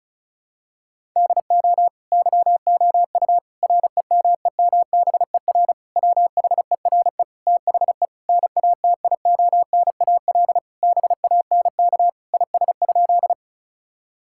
Morse code, heard as a keyed tone, about 36 wpm, 700 Hz. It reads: DO YOU REMEMBER WHERE THE NATIONAL BANK IS?
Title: Sentences from Top 1000 Words - Review Part 6 - 35wpm